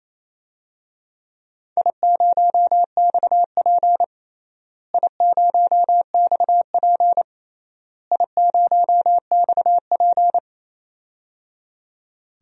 Morse code, heard as a keyed tone, about 27 wpm, 700 Hz. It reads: I0XP I0XP I0XP
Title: DXCC-100 Call Signs - 4 Characters - Repeated 3x - 28wpm